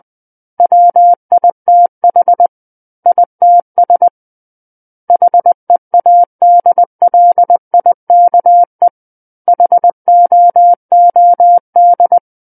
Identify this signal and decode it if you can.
Morse code, heard as a keyed tone, about 20 wpm, 700 Hz. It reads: WITH ITS HEADLIKE HOOD
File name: 20 WPM